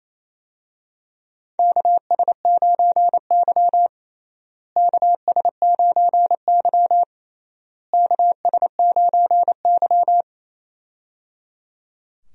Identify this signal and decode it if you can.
Morse code, heard as a keyed tone, about 27 wpm, 700 Hz. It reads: KS9Y KS9Y KS9Y
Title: USA Call Signs - 4 Characters - Repeated 3x - 28wpm